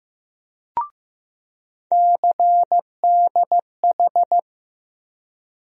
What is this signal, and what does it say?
Morse code, heard as a keyed tone, about 15 wpm, 700 Hz. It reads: CDH
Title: USA Call Sign Postfixes - 3 Letters 15wpm